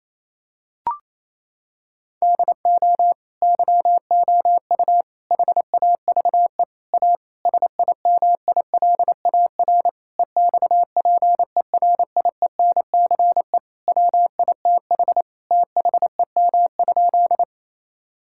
Morse code, heard as a keyed tone, about 28 wpm, 700 Hz. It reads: DO YOU HAVE A SIMILAR EXPERIENCE WITH THEM?
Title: Sentences from Top 1000 Words - Review Part 5 - 28wpm